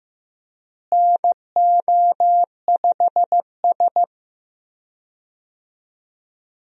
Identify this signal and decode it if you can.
Morse code, heard as a keyed tone, about 15 wpm, 700 Hz. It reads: NO5S
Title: USA Call Signs - 4 Characters - Repeated 2x - 15wpm